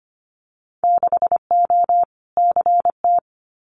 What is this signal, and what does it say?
Morse code, heard as a keyed tone, about 25 wpm, 700 Hz. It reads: 6O CT
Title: Winter Field Day Exchanges Encore 25wpm